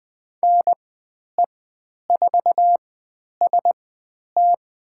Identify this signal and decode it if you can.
Morse code, heard as a keyed tone, about 20 wpm, spaced as for 10 wpm, 700 Hz. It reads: NE4ST
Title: 109 Lesson 18g - Letter m - Call Signs 20wpm